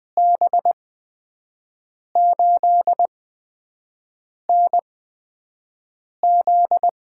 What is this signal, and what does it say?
Morse code, heard as a keyed tone, about 20 wpm, spaced as for 5 wpm, 700 Hz. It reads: B8NZ
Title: USA Call Signs - 5 Characters - 8x Character Spacing - 20wpm